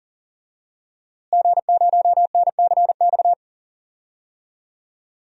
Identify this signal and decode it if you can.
Morse code, heard as a keyed tone, about 40 wpm, 700 Hz. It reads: G0NCX
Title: DXCC-100 Call Signs - 5 Characters - Repeated 3x - 40wpm